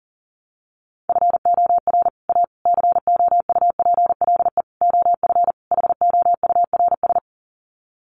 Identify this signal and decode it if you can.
Morse code, heard as a keyed tone, about 40 wpm, 700 Hz. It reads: FOR A COUPLE OF HOURS
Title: Sets of 5 Words Encore2 40wpm